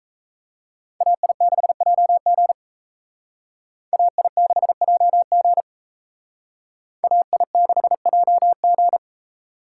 Morse code, heard as a keyed tone, about 36 wpm, 700 Hz. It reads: AI6JG AI6JG AI6JG